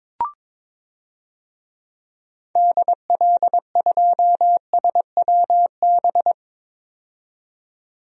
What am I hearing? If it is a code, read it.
Morse code, heard as a keyed tone, about 22 wpm, 700 Hz. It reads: DL2SWB
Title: DXCC-100 Call Signs - 6 Characters - 22wpm